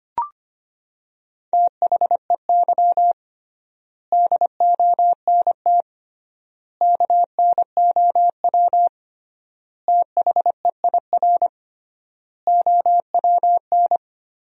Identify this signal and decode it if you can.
Morse code, heard as a keyed tone, about 25 wpm, 700 Hz. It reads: THEY DONT KNOW THEIR OWN
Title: 3x Word-Spacing - Sentences from Top 300 Words - 25wpm